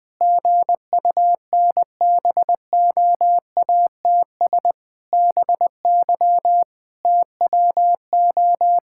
Morse code, heard as a keyed tone, about 21 wpm, 700 Hz. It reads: GUNBOATS BY TWO
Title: Pangram Challenge v2 - 20wpm